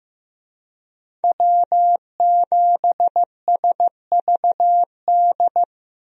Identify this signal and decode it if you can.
Morse code, heard as a keyed tone, about 15 wpm, 700 Hz. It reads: W7SVD
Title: USA Call Signs - 5 Characters - Repeated 2x - 15wpm